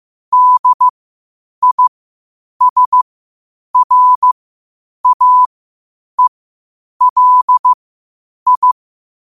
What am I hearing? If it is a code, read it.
Morse code, heard as a keyed tone, about 15 wpm, spaced as for 8 wpm, 1000 Hz. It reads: DISRAELI